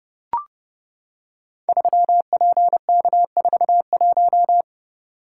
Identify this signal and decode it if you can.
Morse code, heard as a keyed tone, about 32 wpm, 700 Hz. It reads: 3PK41
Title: Code Groups - 5 Letter:Numbers - 30wpm